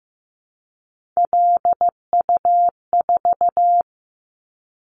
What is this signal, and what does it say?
Morse code, heard as a keyed tone, about 15 wpm, 700 Hz. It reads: LU4
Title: Callsign Prefixes for Top 50 DXCC - 15wpm